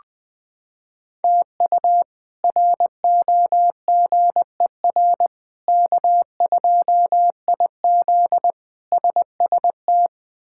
Morse code, heard as a keyed tone, about 20 wpm, 700 Hz. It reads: TU ROGER K2IZ SST
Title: K1USN SST QSO Practice - 20wpm